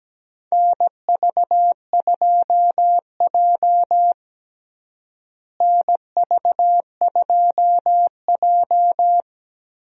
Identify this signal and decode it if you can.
Morse code, heard as a keyed tone, about 17 wpm, 700 Hz. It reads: NV2J NV2J
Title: USA Call Signs - 4 Characters - Repeated 3x - 17wpm